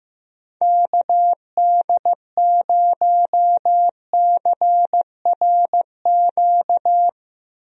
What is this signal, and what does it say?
Morse code, heard as a keyed tone, about 15 wpm, 700 Hz. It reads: KD0CRQ
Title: USA Call Signs - 6 Characters - Repeated 3x - 15wpm